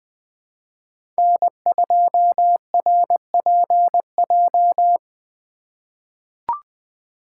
Morse code, heard as a keyed tone, about 20 wpm, 700 Hz. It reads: N2RPJ
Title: USA Call Signs - 2x Character Spacing - 20wpm